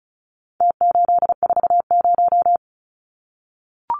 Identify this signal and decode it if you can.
Morse code, heard as a keyed tone, about 34 wpm, 700 Hz. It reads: T840
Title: Numbers - Contest Style - 4-Digit 35wpm